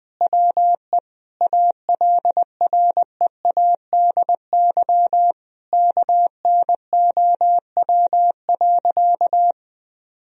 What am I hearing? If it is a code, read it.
Morse code, heard as a keyed tone, about 20 wpm, 700 Hz. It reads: WE ALREADY KNOW.
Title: Sentences from Top 1000 Words - Review Part 1 - 20wpm